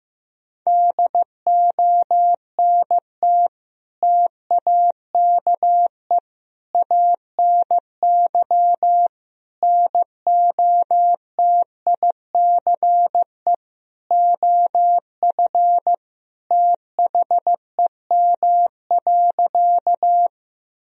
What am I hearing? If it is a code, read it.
Morse code, heard as a keyed tone, about 15 wpm, 700 Hz. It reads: DONT TAKE ANY NOTICE OF THEM.
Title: Sentences from Top 1000 Words - Review Part 2 - 15wpm